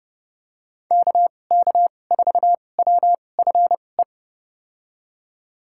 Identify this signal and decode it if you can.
Morse code, heard as a keyed tone, about 30 wpm, spaced as for 22 wpm, 700 Hz. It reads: KK4WFE
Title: USA Call Signs - 6 Characters - 2x Character Spacing - 30wpm